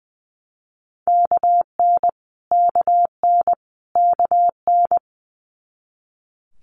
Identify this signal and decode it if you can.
Morse code, heard as a keyed tone, about 20 wpm, 700 Hz. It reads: KN KN KN
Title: Multi-letter Phonemes Repeated 3x 20wpm